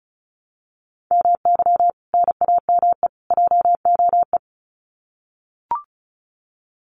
Morse code, heard as a keyed tone, about 34 wpm, 700 Hz. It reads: MY NAME JOE
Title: Common QSO Phrases 35wpm